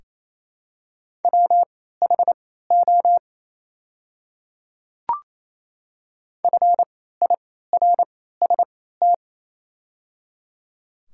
Morse code, heard as a keyed tone, about 28 wpm, spaced as for 16 wpm, 700 Hz. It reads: WHO FIRST
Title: 3x Character Spacing - Top 100 Words - 28wpm